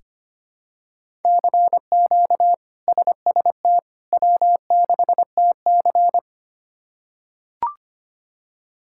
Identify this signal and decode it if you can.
Morse code, heard as a keyed tone, about 25 wpm, 700 Hz. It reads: CQ SST W6TC
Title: K1USN SST QSO Practice - 25wpm